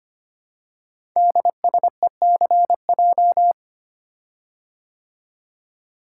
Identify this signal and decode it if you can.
Morse code, heard as a keyed tone, about 25 wpm, 700 Hz. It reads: DSECJ
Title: Code Groups - 5 Letters - 25wpm